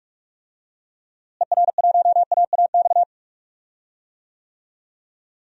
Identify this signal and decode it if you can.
Morse code, heard as a keyed tone, about 46 wpm, 700 Hz. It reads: ER1AAX
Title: DXCC-100 Call Signs - 6 Characters - Repeated 2x - 45wpm